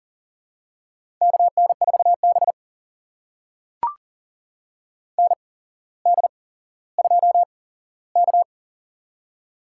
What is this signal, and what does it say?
Morse code, heard as a keyed tone, about 40 wpm, spaced as for 17 wpm, 690 Hz. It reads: KN4B ND2K
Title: USA Call Signs - 4 Characters - 8x Character Spacing - 40wpm